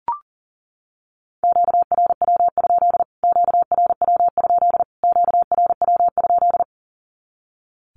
Morse code, heard as a keyed tone, about 40 wpm, 700 Hz. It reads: QRW? QRW? QRW?
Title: Q-Codes Repeated 3x 40wpm